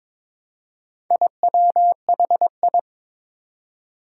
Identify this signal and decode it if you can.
Morse code, heard as a keyed tone, about 21 wpm, 700 Hz. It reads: IWHI
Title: Four Challenging Letter-Number Combinations 22wpm